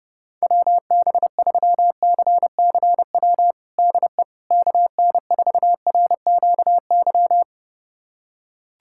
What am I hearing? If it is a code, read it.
Morse code, heard as a keyed tone, about 32 wpm, 700 Hz. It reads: WB3CCW DE KN4RQY
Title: USA Callsign DE Callsign - 6 Characters - 30wpm